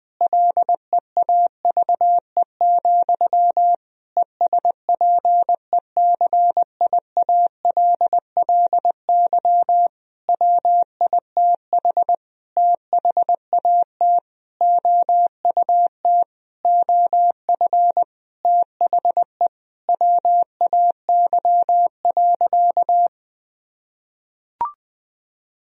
Morse code, heard as a keyed tone, about 21 wpm, 700 Hz. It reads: LEAVE, ESPECIALLY WITH THAT OUT OF THE WAY.
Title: Sentences from Top 1000 Words - Review Part 4 - 20wpm